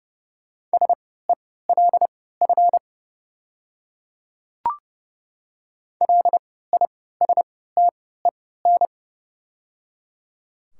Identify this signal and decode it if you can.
Morse code, heard as a keyed tone, about 31 wpm, spaced as for 17 wpm, 690 Hz. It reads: SELF LISTEN
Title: 3x Character Spacing - Top 500 Words - 30wpm